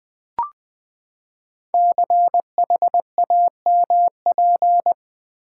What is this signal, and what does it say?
Morse code, heard as a keyed tone, about 20 wpm, 700 Hz. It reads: CHAMP
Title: Build A Word - 5 Letters At A Time - 12-Letter Words - 20wpm